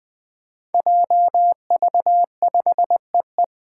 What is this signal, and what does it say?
Morse code, heard as a keyed tone, about 20 wpm, 700 Hz. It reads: JV5EE
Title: Five Challenging Letter-Number Combinations 20wpm